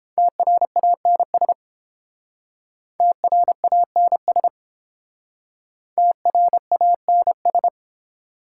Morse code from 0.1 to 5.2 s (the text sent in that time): TRANS TRANS